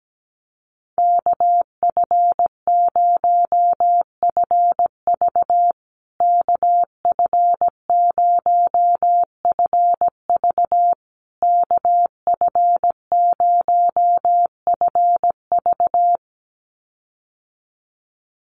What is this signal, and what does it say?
Morse code, heard as a keyed tone, about 17 wpm, 700 Hz. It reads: KF0FV KF0FV KF0FV
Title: USA Call Signs Repeated 3x 17wpm